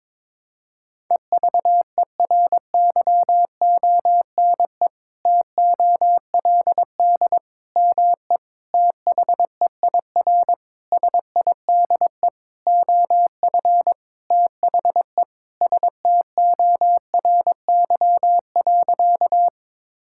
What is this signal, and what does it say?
Morse code, heard as a keyed tone, about 22 wpm, 700 Hz. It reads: EVERYONE TOLD ME THEIR SIDE OF THE STORY.
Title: Sentences from Top 1000 Words - Review Part 5 - 22wpm